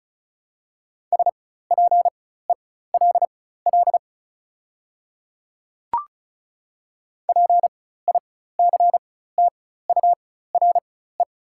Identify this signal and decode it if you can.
Morse code, heard as a keyed tone, about 36 wpm, spaced as for 16 wpm, 700 Hz. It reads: SPELL PICTURE
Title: 4x Character Spacing - Top 200 Words - 35wpm